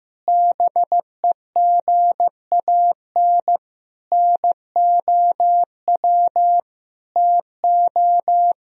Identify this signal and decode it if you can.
Morse code, heard as a keyed tone, about 15 wpm, 700 Hz. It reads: BEGAN NOW TO
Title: Sentences from Top 1000 Words - Review Part 3 - 15wpm